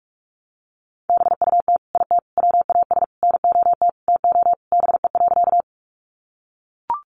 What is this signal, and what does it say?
Morse code, heard as a keyed tone, about 44 wpm, 700 Hz. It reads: BUT IT WAS NOT TO BE.
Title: Sentences from Top 600 Words 45wpm